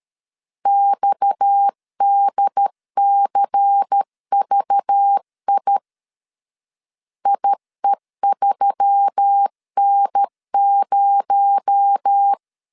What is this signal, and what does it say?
Morse code, heard as a keyed tone, about 13 wpm, 800 Hz. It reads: XDCVI IE3N0
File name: CW / DET / 1 6 1500C S C 12wpm 3533